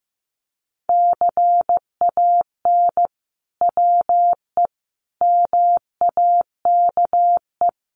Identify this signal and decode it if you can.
Morse code, heard as a keyed tone, about 15 wpm, 700 Hz. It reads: CAN WE MAKE
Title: Sentences from Top 1500 Words 15wpm